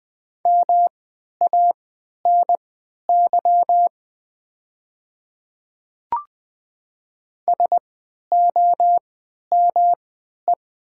Morse code, heard as a keyed tone, about 20 wpm, spaced as for 11 wpm, 700 Hz. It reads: MANY SOME
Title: 3x Character Spacing - Top 300 Words - 20wpm